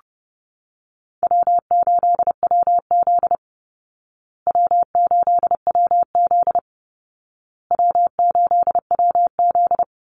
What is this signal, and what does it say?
Morse code, heard as a keyed tone, about 30 wpm, 700 Hz. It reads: W8WZ W8WZ W8WZ
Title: Easy Call Signs Repeated 3x - 30wpm